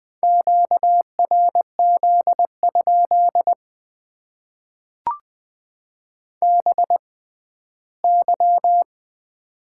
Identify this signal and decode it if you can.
Morse code, heard as a keyed tone, about 20 wpm, spaced as for 14 wpm, 700 Hz. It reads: QRZ? BY
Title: Top 90 QSO Elements - 6x Character Spacing - 20wpm